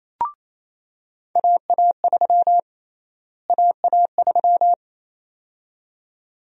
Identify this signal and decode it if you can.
Morse code, heard as a keyed tone, about 28 wpm, 700 Hz. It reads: AA3 AA3
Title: USA Call Sign Prefixes - Repeated 2x - 28wpm